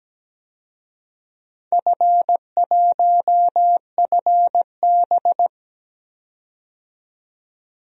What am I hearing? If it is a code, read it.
Morse code, heard as a keyed tone, about 17 wpm, 700 Hz. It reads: F1FB
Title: DXCC-100 Call Signs - 4 Characters - 17wpm